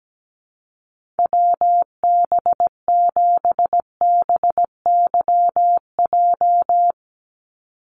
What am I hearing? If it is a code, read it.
Morse code, heard as a keyed tone, about 17 wpm, 700 Hz. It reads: WB7BYJ
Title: USA Call Signs 17wpm